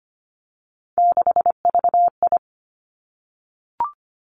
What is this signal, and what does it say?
Morse code, heard as a keyed tone, about 25 wpm, 700 Hz. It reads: 6VI
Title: Three Challenging Letter-Number Combinations 25wpm